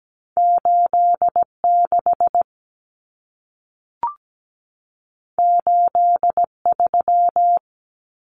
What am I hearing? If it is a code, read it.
Morse code, heard as a keyed tone, about 17 wpm, 700 Hz. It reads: 86 83
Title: Numbers - 2-Digit 17wpm